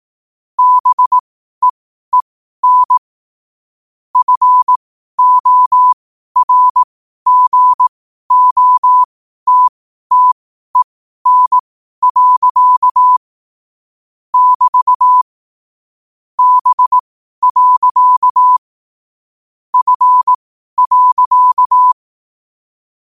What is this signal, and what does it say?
Morse code, heard as a keyed tone, about 18 wpm, spaced as for 12 wpm, 1000 Hz. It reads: BEEN FORGOTTEN. <BT> B. F.
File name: qotdfiles / 12wpm